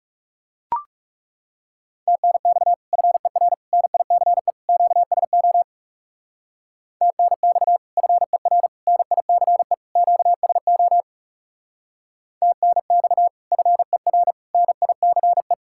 Morse code, heard as a keyed tone, about 36 wpm, 700 Hz. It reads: TNX FER NICE QSO TNX FER NICE QSO TNX FER NICE
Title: Speed-Racing Common QSO Phrases - 30wpm